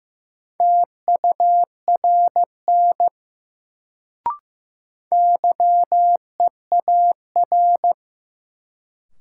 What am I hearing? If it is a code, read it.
Morse code, heard as a keyed tone, about 15 wpm, 700 Hz. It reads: TURN YEAR
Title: Sentences from Top 200 Words - New Word Drill - 15wpm